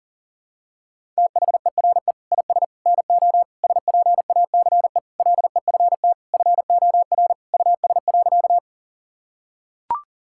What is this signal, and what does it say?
Morse code, heard as a keyed tone, about 40 wpm, 700 Hz. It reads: THERE IS NO SPACE LEFT FOR US.
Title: Sentences from Top 1000 Words - Review Part 5 - 40wpm